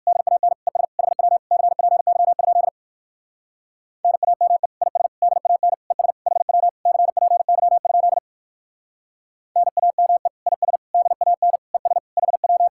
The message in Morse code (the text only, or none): DAN ES HW CPY? NAME IS DAN ES HW CPY? NAME IS DAN ES HW